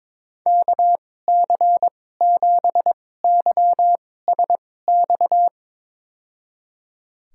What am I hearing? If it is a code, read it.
Morse code, heard as a keyed tone, about 22 wpm, spaced as for 16 wpm, 700 Hz. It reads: KC7YSX
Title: USA Call Signs - 2x Character Spacing - 22wpm